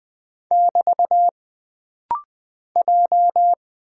Morse code, heard as a keyed tone, about 20 wpm, 700 Hz. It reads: <BT> J
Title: Single Character (Letter:Number:Punctuation:Prosign) - Rapid-Fire - 20wpm